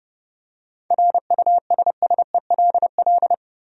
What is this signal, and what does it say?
Morse code, heard as a keyed tone, about 32 wpm, 700 Hz. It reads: RUSSELL
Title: USA Common Names - 30wpm